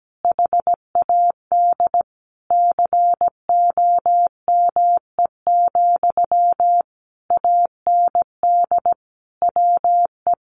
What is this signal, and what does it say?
Morse code, heard as a keyed tone, about 17 wpm, 700 Hz. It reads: HAD COME, AND WE
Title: Sentences from Top 1500 Words 17wpm